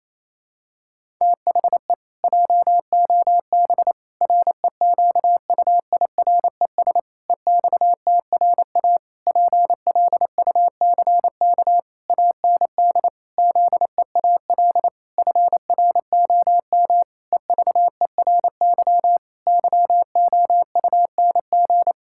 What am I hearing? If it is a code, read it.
Morse code, heard as a keyed tone, about 28 wpm, 700 Hz. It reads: THE JOB REQUIRES EXTRA PLUCK AND ZEAL FROM EVERY YOUNG
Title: Pangram Challenge v2 - 28wpm